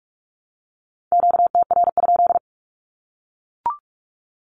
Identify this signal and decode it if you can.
Morse code, heard as a keyed tone, about 44 wpm, 700 Hz. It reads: QTR?
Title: Q-Codes Repeated 3x 45wpm